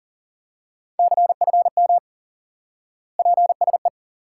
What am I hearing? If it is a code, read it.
Morse code, heard as a keyed tone, about 37 wpm, 700 Hz. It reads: CFM PSE
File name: IQE Course 3 - Lesson 89 - YAGI - Two Elements - 40wpm